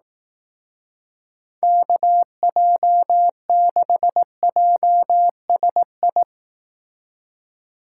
Morse code, heard as a keyed tone, about 18 wpm, 700 Hz. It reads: KJ6JSI
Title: Speed Racing - USA 2x3 Call Signs 15wpm